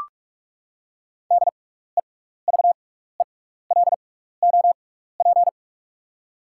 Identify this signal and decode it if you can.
Morse code, heard as a keyed tone, about 44 wpm, spaced as for 15 wpm, 700 Hz. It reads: DEVELOP
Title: 6x Character Spacing - Top 500 Words - 45wpm